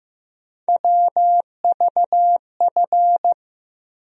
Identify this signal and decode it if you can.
Morse code, heard as a keyed tone, about 15 wpm, 700 Hz. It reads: WVF
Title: USA Call Sign Postfixes - 3 Letters 15wpm